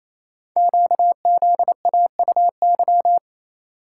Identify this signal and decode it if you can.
Morse code, heard as a keyed tone, about 27 wpm, 700 Hz. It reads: QZAUY
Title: Code Groups - 5 Letters - 28wpm